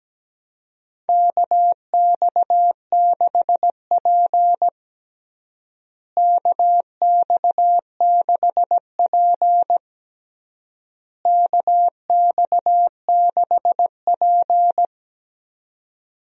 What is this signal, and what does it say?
Morse code, heard as a keyed tone, about 17 wpm, 700 Hz. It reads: KX6P KX6P KX6P
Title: USA Call Signs - 4 Characters - Repeated 3x - 17wpm